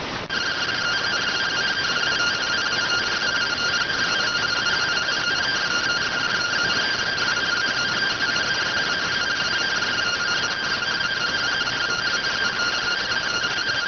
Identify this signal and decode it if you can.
Two-tone teleprinter signal, about 170 Hz shift.